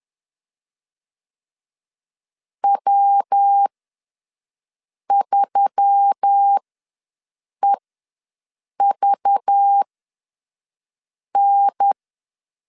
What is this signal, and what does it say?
Morse code, heard as a keyed tone, about 10 wpm, spaced as for 5 wpm, 800 Hz. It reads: W3EVN
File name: LEZIONI / 6 PKX0 / 6 PK ws 10wpm 3745